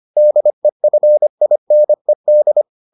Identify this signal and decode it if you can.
Morse code, heard as a keyed tone, about 25 wpm, 600 Hz. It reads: DEFINED